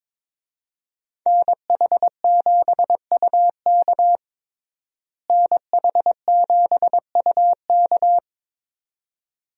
Morse code, heard as a keyed tone, about 22 wpm, 700 Hz. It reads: NH7UK NH7UK
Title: USA Call Signs - 5 Characters - Repeated 2x - 22wpm